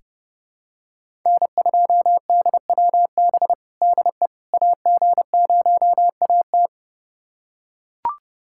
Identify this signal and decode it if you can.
Morse code, heard as a keyed tone, about 30 wpm, 700 Hz. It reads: N2DWB DE AG0AT
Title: USA Callsign DE Callsign - 5 Characters - 30wpm